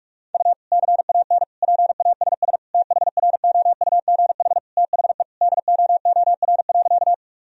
Morse code, heard as a keyed tone, about 46 wpm, 700 Hz. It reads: U CAN PASS THROUGH THE DOOR.
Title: Sentences from Top 400 Words and Abbreviations 45wpm